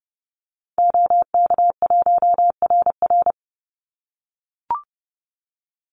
Morse code, heard as a keyed tone, about 30 wpm, 700 Hz. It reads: OK1RR
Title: Call Signs - 30wpm